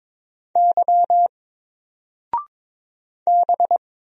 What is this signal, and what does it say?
Morse code, heard as a keyed tone, about 22 wpm, 700 Hz. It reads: Y B
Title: Single Letter-Number 22wpm